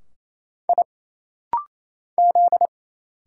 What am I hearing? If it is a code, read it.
Morse code, heard as a keyed tone, about 27 wpm, 700 Hz. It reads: I Z
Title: Single Letters - Rapid-Fire 28wpm